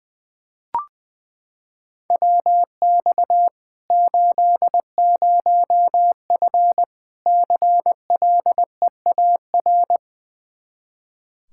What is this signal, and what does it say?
Morse code, heard as a keyed tone, about 20 wpm, 700 Hz. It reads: WX 80F CLEAR
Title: Practice QSO - Exchange 2 - Info - 20wpm